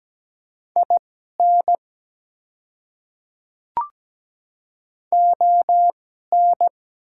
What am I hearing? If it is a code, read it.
Morse code, heard as a keyed tone, about 17 wpm, spaced as for 12 wpm, 700 Hz. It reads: IN ON
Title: 2x Character Spacing - Top 100 Words - 17wpm